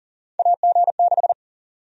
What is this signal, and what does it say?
Morse code, heard as a keyed tone, about 40 wpm, 700 Hz. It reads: AG6
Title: USA Call Sign Prefixes - Repeated 2x - 40wpm